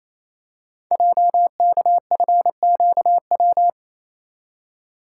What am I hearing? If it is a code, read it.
Morse code, heard as a keyed tone, about 27 wpm, 700 Hz. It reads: JKFQW
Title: Code Groups - 5 Letters - 28wpm